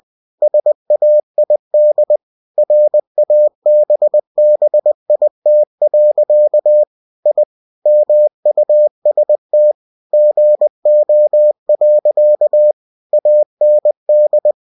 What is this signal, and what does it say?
Morse code, heard as a keyed tone, about 20 wpm, 600 Hz. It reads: SAID RABBIT. I MUST GO. AND
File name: cw-books / winnie-the-pooh / winnie-20wpm-page-0080